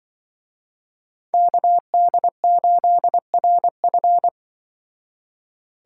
Morse code, heard as a keyed tone, about 24 wpm, 700 Hz. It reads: KD8RF
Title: Speed Racing - USA 2x2 Call Signs 20wpm